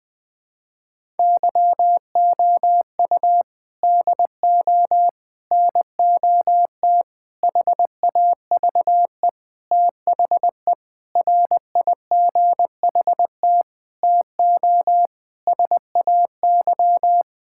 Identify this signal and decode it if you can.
Morse code, heard as a keyed tone, about 20 wpm, 700 Hz. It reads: YOU DO NOT HAVE THE RIGHT TO SAY
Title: Sentences with 5-Word Phrases - Spoken Beginning - 20wpm